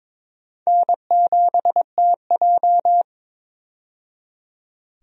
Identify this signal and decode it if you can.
Morse code, heard as a keyed tone, about 22 wpm, 700 Hz. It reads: N7TJ
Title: USA Call Signs - 4 Characters - 22wpm